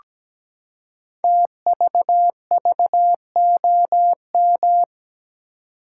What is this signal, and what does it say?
Morse code, heard as a keyed tone, about 17 wpm, 700 Hz. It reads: TVVOM
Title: Code Groups - 5 Letters - 17wpm